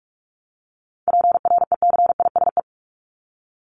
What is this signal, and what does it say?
Morse code, heard as a keyed tone, about 44 wpm, 700 Hz. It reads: PRECISE
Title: LY Suffix - Top 100 Words with Base Words 45wpm